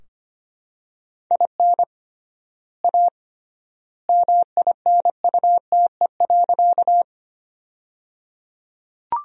Morse code, heard as a keyed tone, about 25 wpm, 700 Hz. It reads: IN A MINUTE.
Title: 3x Word-Spacing - Sentences from Top 500 Words - 25wpm